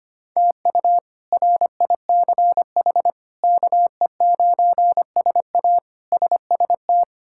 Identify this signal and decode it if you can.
Morse code, heard as a keyed tone, about 25 wpm, 700 Hz. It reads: TU RICH KE9SA SST
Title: K1USN SST QSO Practice - 25wpm